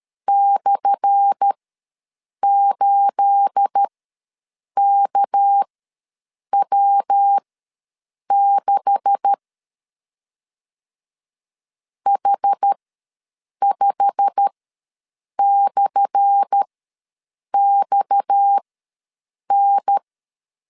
Morse code, heard as a keyed tone, about 13 wpm, spaced as for 7 wpm, 800 Hz. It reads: /8KW6 H5/XN
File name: LEZIONI / 10 SIGNS / 10A SLASH ws 12wpm 3443